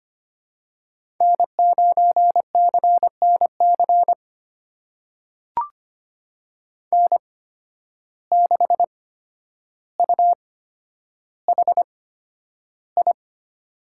Morse code, heard as a keyed tone, about 25 wpm, spaced as for 11 wpm, 700 Hz. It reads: N9CNC N6UHI
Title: USA Call Signs - 5 Characters - 8x Character Spacing - 25wpm